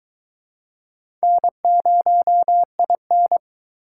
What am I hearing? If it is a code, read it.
Morse code, heard as a keyed tone, about 20 wpm, 700 Hz. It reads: N0IN